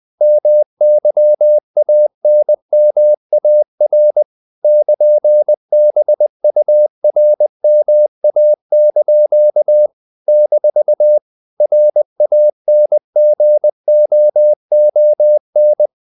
Morse code, heard as a keyed tone, about 20 wpm, 600 Hz. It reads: MYANMAR <KN>BURMA) - RANGOON
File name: CW / Practice / NationalCapitals20